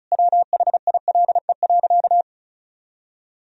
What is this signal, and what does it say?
Morse code, heard as a keyed tone, about 36 wpm, 700 Hz. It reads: WHILE.
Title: Sentences from Top 1000 Words - Review Part 1 - 35wpm